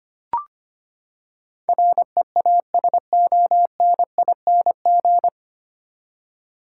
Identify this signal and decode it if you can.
Morse code, heard as a keyed tone, about 25 wpm, 700 Hz. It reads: REASONING
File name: Nine Letter Words - 25wpm